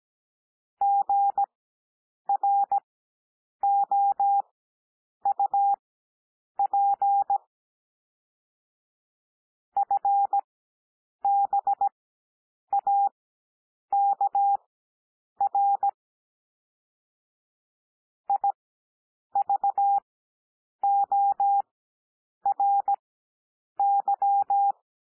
Morse code, heard as a keyed tone, about 17 wpm, spaced as for 8 wpm, 800 Hz. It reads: GROUP FBAKR IVORY
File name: cwmp3 / L2 SOUND LETTERS 17 wpm